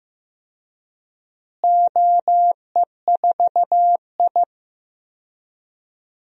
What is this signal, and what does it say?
Morse code, heard as a keyed tone, about 15 wpm, 700 Hz. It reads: OE4I
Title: DXCC-100 Call Signs - 4 Characters - Repeated 3x - 15wpm